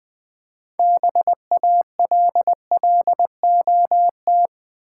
Morse code, heard as a keyed tone, about 20 wpm, 700 Hz. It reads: BALLOT
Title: 6 Letter Words - First 3 Letters Given - 20wpm